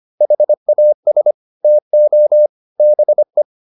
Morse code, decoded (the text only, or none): HAS TO BE